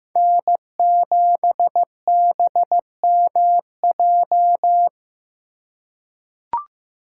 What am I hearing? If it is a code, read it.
Morse code, heard as a keyed tone, about 15 wpm, 700 Hz. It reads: N7BMJ
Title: USA Call Signs - 5 Characters - 2x Character Spacing - 15wpm